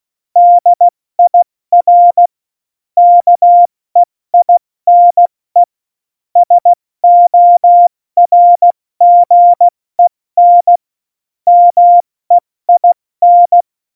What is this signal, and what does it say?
Morse code, heard as a keyed tone, about 16 wpm, 700 Hz. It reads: DIR KEINE SORGEN MEIN
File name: Morsekurs / Witze / Witz 25. Juni 2019, 70 F80